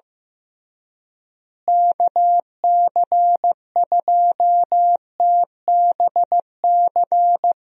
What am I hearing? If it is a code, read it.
Morse code, heard as a keyed tone, about 15 wpm, 700 Hz. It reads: KC2TBC
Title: USA Call Signs - 6 Characters - Repeated 3x - 15wpm